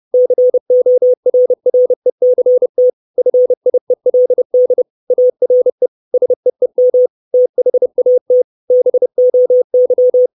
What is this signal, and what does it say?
Morse code, heard as a keyed tone, about 30 wpm, 500 Hz. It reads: CORRECT FIELD ARE SEEM THAT BOY
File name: 632 Words 30WPM